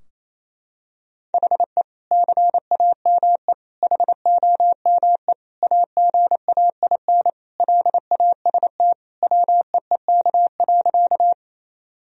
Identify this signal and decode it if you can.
Morse code, heard as a keyed tone, about 28 wpm, 700 Hz. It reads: HE CAME HOME AGAIN LAST WEEK.
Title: Sentences from Top 1000 Words - Review Part 2 - 28wpm